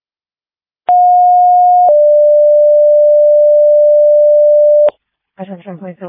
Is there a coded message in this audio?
Yes: two-tone page from 0.9 to 4.9 s, heard as a one-second tone, then a three-second tone, about 720 then 590 Hz, with speech.